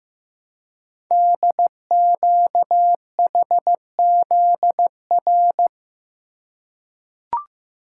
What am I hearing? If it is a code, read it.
Morse code, heard as a keyed tone, about 15 wpm, 700 Hz. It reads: DQHZR
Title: Code Groups - 5 Letter:Numbers - 15wpm